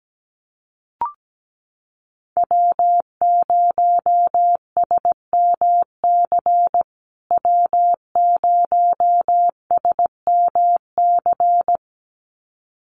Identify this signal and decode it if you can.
Morse code, heard as a keyed tone, about 17 wpm, 700 Hz. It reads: W0SMC W0SMC
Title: USA Call Signs Repeated 2x 17wpm